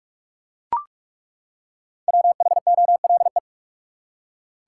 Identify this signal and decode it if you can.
Morse code, heard as a keyed tone, about 42 wpm, 700 Hz. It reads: WHOLE
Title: Five Letter Words - 45wpm